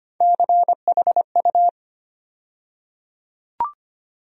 Morse code, heard as a keyed tone, about 25 wpm, 700 Hz. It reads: CHU
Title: USA Call Sign Postfixes - 3 Letters 25wpm